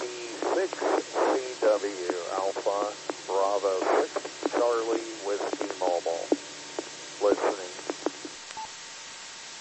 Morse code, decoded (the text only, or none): none